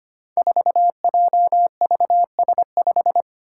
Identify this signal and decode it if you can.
Morse code, heard as a keyed tone, about 25 wpm, 700 Hz. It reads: 4JVS5
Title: Five Challenging Letter-Number Combinations 25wpm